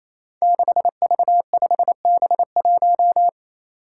Morse code, heard as a keyed tone, about 28 wpm, 700 Hz. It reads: 6V5B1
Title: Five Challenging Letter-Number Combinations 28wpm